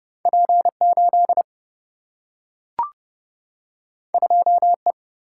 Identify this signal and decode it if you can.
Morse code, heard as a keyed tone, about 30 wpm, 700 Hz. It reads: P 8 2 E
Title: Single-Letter Single-Number 30wpm